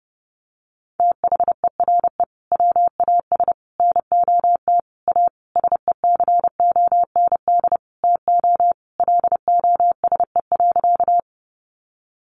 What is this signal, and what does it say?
Morse code, heard as a keyed tone, about 30 wpm, 700 Hz. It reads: THERE WAS NOT A SECOND TO LOSE.
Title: Sentences from Top 1100 Words 30wpm